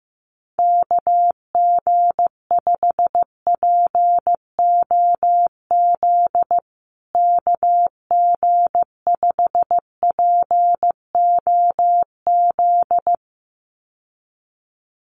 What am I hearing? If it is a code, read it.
Morse code, heard as a keyed tone, about 15 wpm, 700 Hz. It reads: KG5POZ KG5POZ
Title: USA Call Signs Repeated 2x 15wpm